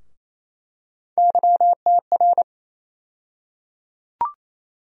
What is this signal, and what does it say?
Morse code, heard as a keyed tone, about 27 wpm, 700 Hz. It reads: YTR
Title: USA Call Sign Postfixes - 3 Letters 28wpm